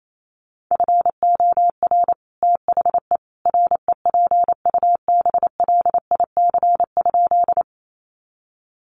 Morse code, heard as a keyed tone, about 28 wpm, 700 Hz. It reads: FOR THE REPUBLIC?
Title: Sentences from Top 1500 Words 28wpm